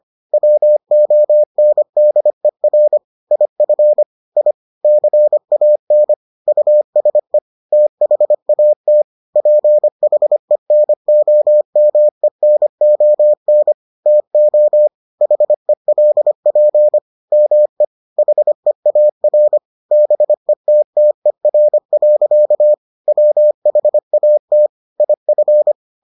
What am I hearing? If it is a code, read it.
Morse code, heard as a keyed tone, about 25 wpm, 600 Hz. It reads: WONDER IF I CAN USE THAT PHENOMENON TO HELP ME HEAR BETTER. WHAT IF